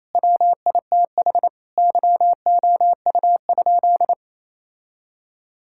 Morse code, heard as a keyed tone, about 28 wpm, 700 Hz. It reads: WITH YOU?
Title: Sentences from Top 1000 Words - Review Part 5 - 28wpm